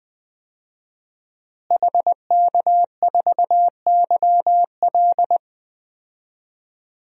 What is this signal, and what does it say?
Morse code, heard as a keyed tone, about 20 wpm, 700 Hz. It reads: HK4YL
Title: DXCC-100 Call Signs - 5 Characters - Repeated 3x - 20wpm